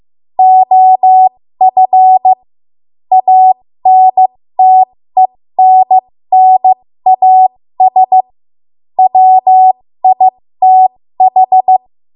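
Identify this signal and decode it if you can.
Morse code, heard as a keyed tone, about 15 wpm, spaced as for 13 wpm, 750 Hz. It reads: OF ANTENNAS WITH
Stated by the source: 13 wpm Morse code practice archive transmission.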